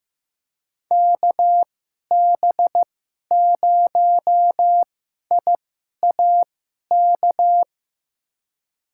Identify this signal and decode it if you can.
Morse code, heard as a keyed tone, about 15 wpm, spaced as for 11 wpm, 700 Hz. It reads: KB0IAK
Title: USA Call Signs - 6 Characters - 2x Character Spacing - 15wpm